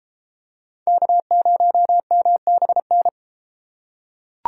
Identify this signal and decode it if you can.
Morse code, heard as a keyed tone, about 32 wpm, 700 Hz. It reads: K0MBN